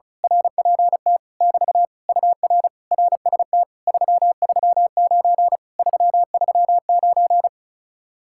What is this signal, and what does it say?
Morse code, heard as a keyed tone, about 33 wpm, 700 Hz. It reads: RPT <BT> UR RST 339 339
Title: Speed-Racing Common QSO Phrases - 35wpm